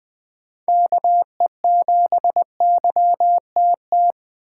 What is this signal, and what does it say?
Morse code, heard as a keyed tone, about 20 wpm, 700 Hz. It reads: KE7YTT
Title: USA Call Signs - 6 Characters - 6x Character Spacing - 20wpm